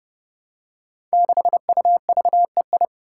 Morse code, heard as a keyed tone, about 30 wpm, 700 Hz. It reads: 6UVEI
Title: Five Challenging Letter-Number Combinations 30wpm